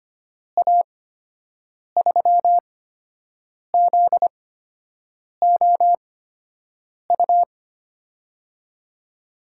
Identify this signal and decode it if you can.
Morse code, heard as a keyed tone, about 25 wpm, spaced as for 7 wpm, 700 Hz. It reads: A3ZOU
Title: USA Call Signs - 8x Character Spacing - 25wpm